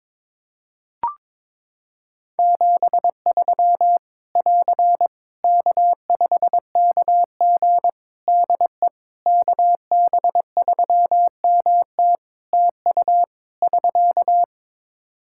Morse code, heard as a keyed tone, about 22 wpm, 700 Hz. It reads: 73 <AR> K5KG DE KB3MT TU <SK>
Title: K1USN SST QSO Practice 22wpm